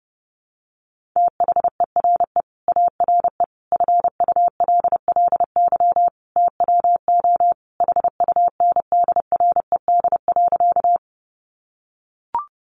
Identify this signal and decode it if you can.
Morse code, heard as a keyed tone, about 30 wpm, 700 Hz. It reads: THERE ARE FULLY TWO HUNDRED.
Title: Sentences from Top 1300 Words 30wpm